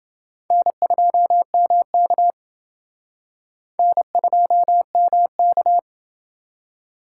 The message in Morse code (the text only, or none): N2MK N2MK